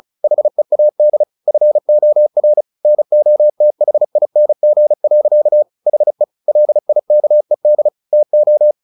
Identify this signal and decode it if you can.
Morse code, heard as a keyed tone, about 34 wpm, 600 Hz. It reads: HEAD FOR NOTHING. HE LIKED TO
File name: the old man and the sea 2 35wpm